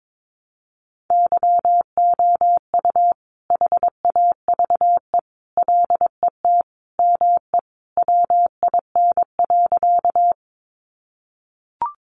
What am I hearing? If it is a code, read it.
Morse code, heard as a keyed tone, about 22 wpm, 700 Hz. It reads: YOU HAVE LET ME WIN.
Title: Sentences from Top 1500 Words 22wpm